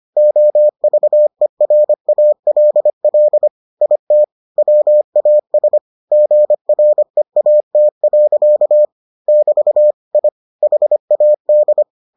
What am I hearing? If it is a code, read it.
Morse code, heard as a keyed tone, about 25 wpm, 600 Hz. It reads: OVERALL IT WAS GREAT. <BT> I HAD